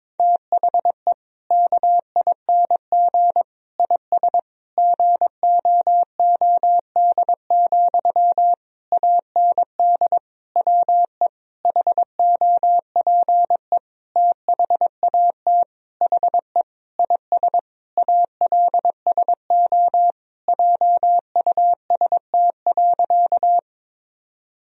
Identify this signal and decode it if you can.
Morse code, heard as a keyed tone, about 22 wpm, 700 Hz. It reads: THE KING IS GOOD, AND WE HOPE THAT HE IS ALSO JUST.
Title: Sentences from Top 1000 Words - Review Part 1 - 22wpm